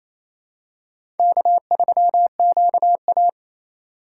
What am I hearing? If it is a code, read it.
Morse code, heard as a keyed tone, about 28 wpm, 700 Hz. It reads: K3QA
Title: USA Call Signs - 4 Characters - 2x Character Spacing - 28wpm